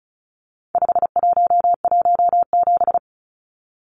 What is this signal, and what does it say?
Morse code, heard as a keyed tone, about 34 wpm, 700 Hz. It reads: 5117
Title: Numbers - 4-Digit 35wpm